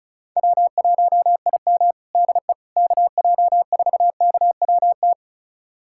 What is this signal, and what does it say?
Morse code, heard as a keyed tone, about 34 wpm, 700 Hz. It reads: W1IM DE KJ4KWT
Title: USA Callsign DE Callsign 35wpm